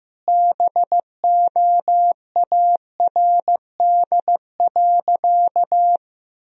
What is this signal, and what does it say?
Morse code, heard as a keyed tone, about 15 wpm, 700 Hz. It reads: BOARD.
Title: Sentences from Top 1000 Words - Review Part 4 - 15wpm